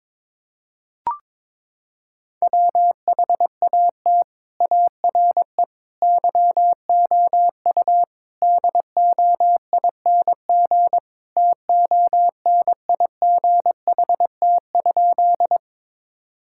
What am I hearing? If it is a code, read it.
Morse code, heard as a keyed tone, about 22 wpm, 700 Hz. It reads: WHAT ARE YOU DOING TONIGHT?
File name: practice / 22wpm / Sentences with 4-Word Phrases - 22wpm